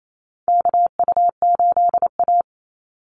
Morse code, heard as a keyed tone, about 28 wpm, 700 Hz. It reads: KU8A
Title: USA Call Signs 28wpm